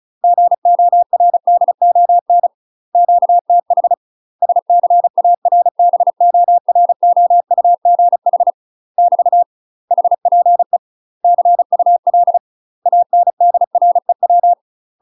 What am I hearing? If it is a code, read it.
Morse code, heard as a keyed tone, about 36 wpm, 700 Hz. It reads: GORDON QTH SCARBOROUGH <BT> HPE CUL ANDREW
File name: UK MASTER QSOs TRAINING / 35WPM / UK MASTER QSOs 0026